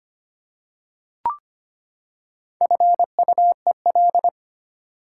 Morse code, heard as a keyed tone, about 25 wpm, 700 Hz. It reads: FUEL
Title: Four Letter Words - 25wpm